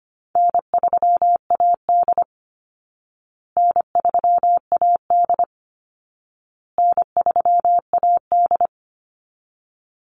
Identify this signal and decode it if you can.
Morse code, heard as a keyed tone, about 25 wpm, 700 Hz. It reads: N3AD N3AD N3AD
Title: Call Signs Repeated 2x - 25wpm